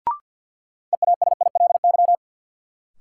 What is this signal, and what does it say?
Morse code, heard as a keyed tone, about 53 wpm, 700 Hz. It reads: EASILY
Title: Sentences from Top 900 Words - New Word Drill - 50wpm